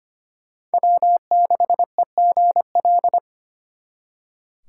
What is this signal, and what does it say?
Morse code, heard as a keyed tone, about 32 wpm, 700 Hz. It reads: W6EGL